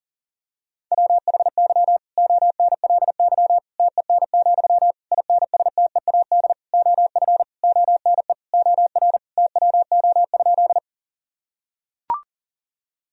Morse code, heard as a keyed tone, about 40 wpm, 700 Hz. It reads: WHY ONLY TEN, INSTEAD OF ONE OR TWO?
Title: Sentences from Top 1000 Words - Review Part 4 - 40wpm